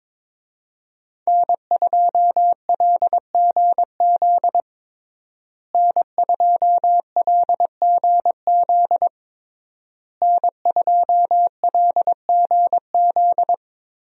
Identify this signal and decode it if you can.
Morse code, heard as a keyed tone, about 21 wpm, 700 Hz. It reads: N2LGZ N2LGZ N2LGZ
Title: USA Call Signs - 5 Characters - Repeated 3x - 22wpm